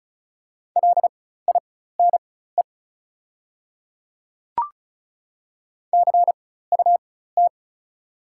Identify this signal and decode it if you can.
Morse code, heard as a keyed tone, about 34 wpm, spaced as for 16 wpm, 690 Hz. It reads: LINE CUT
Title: 4x Character Spacing - Top 300 Words - 35wpm